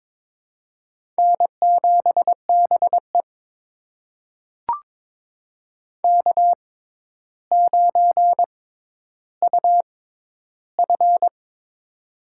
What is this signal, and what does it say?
Morse code, heard as a keyed tone, about 22 wpm, spaced as for 11 wpm, 700 Hz. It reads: N7BE K9UF
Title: USA Call Signs - 4 Characters - 6x Character Spacing - 22wpm